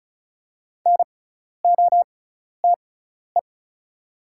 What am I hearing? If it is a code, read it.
Morse code, heard as a keyed tone, about 36 wpm, spaced as for 12 wpm, 700 Hz. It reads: NOTE
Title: 6x Character Spacing - Top 500 Words - 35wpm